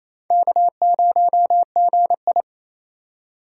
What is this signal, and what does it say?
Morse code, heard as a keyed tone, about 35 wpm, 700 Hz. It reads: K0GI
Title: Speed Racing - USA 1x2 Call Signs 28wpm